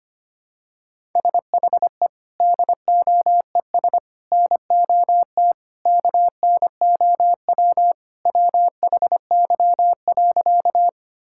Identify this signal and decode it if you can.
Morse code, heard as a keyed tone, about 25 wpm, 700 Hz. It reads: SHE DOES NOT KNOW WHY.
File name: practice / 25wpm / Sentences with 4-Word Phrases - 25wpm